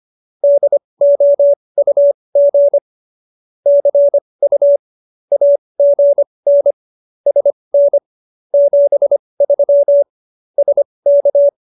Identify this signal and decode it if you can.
Morse code, heard as a keyed tone, about 25 wpm, spaced as for 20 wpm, 570 Hz. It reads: DOUG CU AGN SN 73 SK